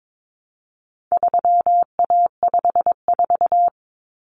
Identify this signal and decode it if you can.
Morse code, heard as a keyed tone, about 22 wpm, 700 Hz. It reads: 3A54
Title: Common Cut Numbers - 4-Digit 22wpm